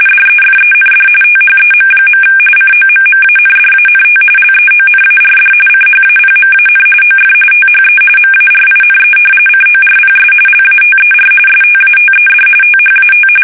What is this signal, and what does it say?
RTTY, about 45.45 baud. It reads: DEFEND THEM. COMMANDANT LEONARD CHAPMAN OF THE MARINE CORPS LATER STATED THAT
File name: fsk850 / battleshipnj